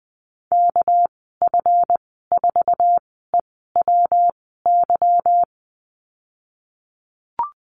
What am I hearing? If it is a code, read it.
Morse code, heard as a keyed tone, about 20 wpm, 700 Hz. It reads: K F 4 E W Y
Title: 238 Lesson 38g - Letter q - Call Signs 20wpm